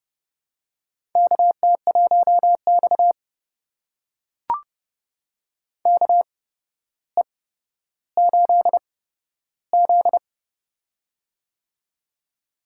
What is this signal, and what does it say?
Morse code, heard as a keyed tone, about 30 wpm, spaced as for 13 wpm, 700 Hz. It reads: KT1X KE8Z
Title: USA Call Signs - 4 Characters - 8x Character Spacing - 30wpm